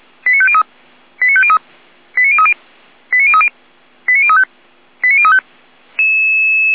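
Five-tone selective call (ZVEI), heard as quick runs of five tones, tones of about 70 ms. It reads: 89503 89503 89030 89030 89035 89035
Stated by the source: ZVEI five-tone selective call received by SDR, address 89140.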